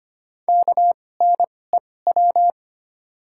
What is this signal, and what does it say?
Morse code, heard as a keyed tone, about 25 wpm, spaced as for 18 wpm, 700 Hz. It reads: KNEW
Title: 2x Character Spacing - Top 500 Words - 25wpm